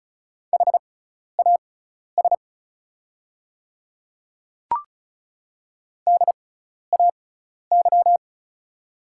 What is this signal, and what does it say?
Morse code, heard as a keyed tone, about 36 wpm, spaced as for 12 wpm, 700 Hz. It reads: HAS DAY
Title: 6x Character Spacing - Top 100 Words - 35wpm